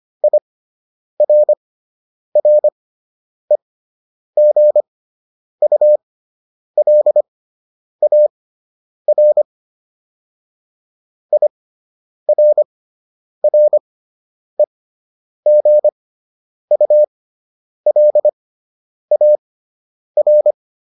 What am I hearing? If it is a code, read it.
Morse code, heard as a keyed tone, about 25 wpm, spaced as for 9 wpm, 600 Hz. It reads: IRREGULAR IRREGULAR